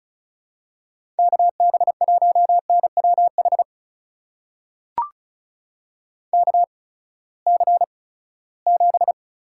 Morse code, heard as a keyed tone, about 34 wpm, spaced as for 20 wpm, 700 Hz. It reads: KB1NWH KC7
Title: USA Call Signs - 6 Characters - 8x Character Spacing - 35wpm